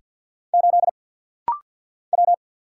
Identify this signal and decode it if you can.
Morse code, heard as a keyed tone, about 50 wpm, 690 Hz. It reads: 8 W
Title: Single Letter-Number Alt - Rapid-Fire 50wpm